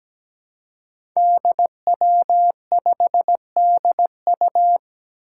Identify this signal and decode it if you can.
Morse code, heard as a keyed tone, about 17 wpm, 700 Hz. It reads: DW5DU
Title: Five Challenging Letter-Number Combinations 17wpm